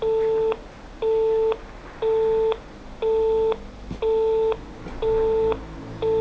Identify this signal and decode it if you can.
Busy tone (France), 440 Hz, with speech.